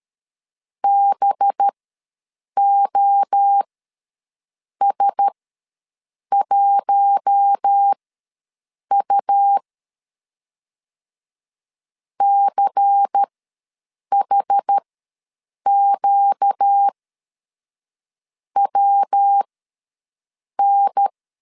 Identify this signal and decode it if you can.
Morse code, heard as a keyed tone, about 13 wpm, spaced as for 6 wpm, 800 Hz. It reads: BOS1U CHQWN
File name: LEZIONI / 5 LBVC / 5 LBVC ws 12wpm 3159